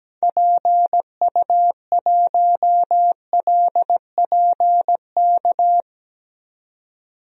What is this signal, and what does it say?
Morse code, heard as a keyed tone, about 17 wpm, 700 Hz. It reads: PU1LPK
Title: DXCC-100 Call Signs - 6 Characters - 17wpm